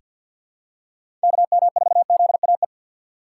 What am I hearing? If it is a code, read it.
Morse code, heard as a keyed tone, about 48 wpm, 700 Hz. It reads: KM4ZAE
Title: USA Call Signs - 6 Characters - 4x Character Spacing - 50wpm